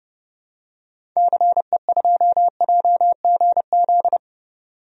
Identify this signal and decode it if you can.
Morse code, heard as a keyed tone, about 32 wpm, 700 Hz. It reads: CE2JGZ
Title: DXCC-100 Call Signs - 6 Characters - 30wpm